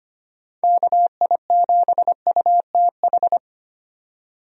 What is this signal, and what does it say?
Morse code, heard as a keyed tone, about 25 wpm, 700 Hz. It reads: KI7UTH
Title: USA Call Signs - 6 Characters - 25wpm